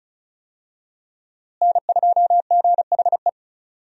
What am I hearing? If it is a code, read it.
Morse code, heard as a keyed tone, about 34 wpm, 700 Hz. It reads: N2GHE
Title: USA Call Signs - 5 Characters - Repeated 3x - 35wpm